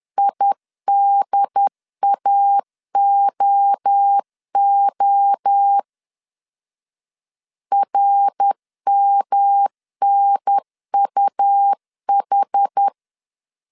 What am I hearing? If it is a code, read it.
Morse code, heard as a keyed tone, about 10 wpm, 800 Hz. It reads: IDAOO RMNUH
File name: CW / DET / 1 3 1500C S C 10wpm 3335